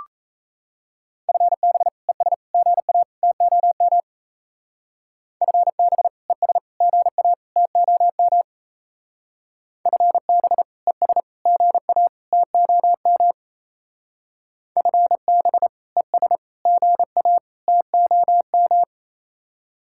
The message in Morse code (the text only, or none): FB ES GA TOM FB ES GA TOM FB ES GA TOM FB ES GA TOM